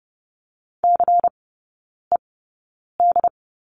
Morse code, heard as a keyed tone, about 30 wpm, spaced as for 9 wpm, 700 Hz. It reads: CED
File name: Lesson 16e - Letter c - Three Characters 30wpm